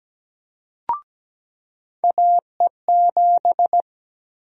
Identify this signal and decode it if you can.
Morse code, heard as a keyed tone, about 17 wpm, 700 Hz. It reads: AE7
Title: USA Call Sign Prefixes - 17wpm